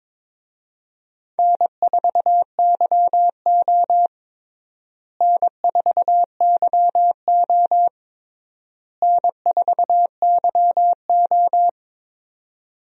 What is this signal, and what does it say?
Morse code, heard as a keyed tone, about 22 wpm, 700 Hz. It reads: N4YO N4YO N4YO
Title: USA Call Signs - 4 Characters - Repeated 3x - 22wpm